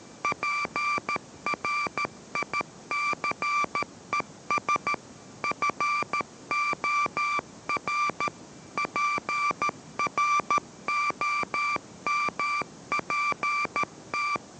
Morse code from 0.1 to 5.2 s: PRICES